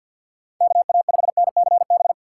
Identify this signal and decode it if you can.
Morse code, heard as a keyed tone, about 53 wpm, 700 Hz. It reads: KA5NCB
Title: USA Call Signs - 2x Character Spacing - 50wpm